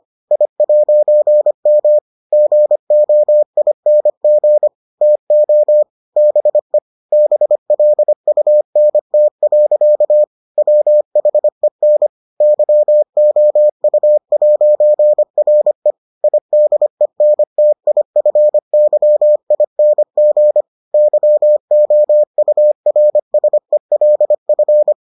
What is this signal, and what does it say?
Morse code, heard as a keyed tone, about 25 wpm, 600 Hz. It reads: I'M GOING TO BE BLUNT. WHEN YOU'RE IDENTIFYING YOURSELF